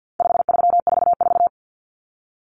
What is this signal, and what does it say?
Morse code, heard as a keyed tone, about 48 wpm, 700 Hz. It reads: 5344
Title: Numbers - 4-Digit 50wpm